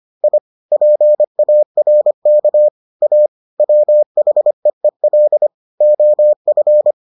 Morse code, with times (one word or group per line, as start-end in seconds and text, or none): 0.2-0.4 I
0.7-2.7 PARK
3.0-3.3 A
3.6-5.5 WHEEL
5.8-6.9 OF